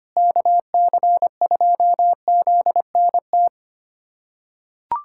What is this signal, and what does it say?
Morse code, heard as a keyed tone, about 25 wpm, 700 Hz. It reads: KC2ZNT
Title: USA Callsign DE Callsign 25wpm